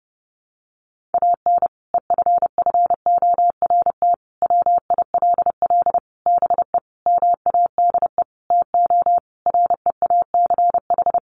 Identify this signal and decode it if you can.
Morse code, heard as a keyed tone, about 30 wpm, 700 Hz. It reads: AN EFFORT WILL BE MADE TO REACH
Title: Sentences from Top 1500 Words 30wpm